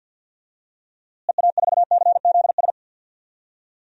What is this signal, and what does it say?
Morse code, heard as a keyed tone, about 50 wpm, 700 Hz. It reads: EA4CZS
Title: DXCC-100 Call Signs - 6 Characters - Repeated 3x - 50wpm